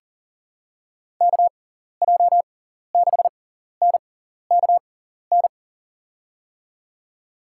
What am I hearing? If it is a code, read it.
Morse code, heard as a keyed tone, about 40 wpm, spaced as for 14 wpm, 700 Hz. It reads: KJ6NKN
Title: USA Call Signs - 6x Character Spacing - 40wpm